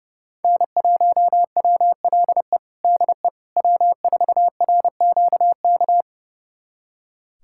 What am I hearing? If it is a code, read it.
Morse code, heard as a keyed tone, about 32 wpm, 700 Hz. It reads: N1WLE DE W4RQK
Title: USA Callsign DE Callsign - 5 Characters - 30wpm